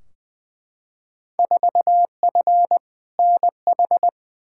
Morse code, heard as a keyed tone, about 20 wpm, 700 Hz. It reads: 4F NH
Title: Field Day Exchanges Encore - 20wpm